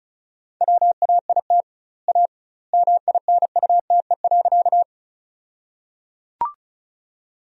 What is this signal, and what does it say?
Morse code, heard as a keyed tone, about 36 wpm, 700 Hz. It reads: WAIT A MINUTE.
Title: 2x Word-Spacing - Sentences from Top 500 Words - 35wpm